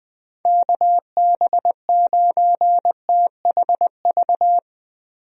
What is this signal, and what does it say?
Morse code, heard as a keyed tone, about 20 wpm, 700 Hz. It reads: KB9THV
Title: USA Call Signs - 6 Characters - Repeated 2x - 20wpm